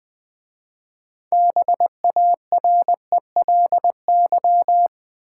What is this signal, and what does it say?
Morse code, heard as a keyed tone, about 20 wpm, 700 Hz. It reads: BARELY
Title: 6 Letter Words - First 3 Letters Given - 20wpm